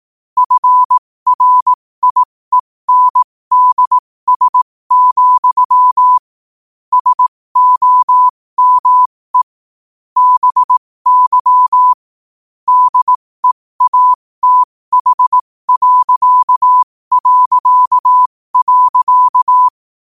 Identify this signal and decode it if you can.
Morse code, heard as a keyed tone, about 18 wpm, spaced as for 15 wpm, 1000 Hz. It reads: FRIENDS, SOME BY DEATH...
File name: qotdfiles / 15wpm